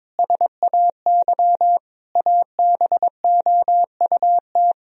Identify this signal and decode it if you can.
Morse code, heard as a keyed tone, about 21 wpm, 700 Hz. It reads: SAY ABOUT
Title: Sentences from Top 1000 Words - Review Part 4 - 22wpm